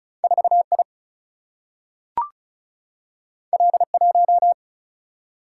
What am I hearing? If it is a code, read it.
Morse code, heard as a keyed tone, about 36 wpm, 700 Hz. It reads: 4 I L 1
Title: Single-Letter Single-Number 35wpm